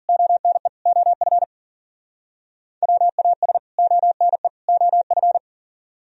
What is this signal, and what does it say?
Morse code, heard as a keyed tone, about 48 wpm, 700 Hz. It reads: ONE OF WAS ONE OF
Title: Speed-Racing - Sets of 3 Words 40wpm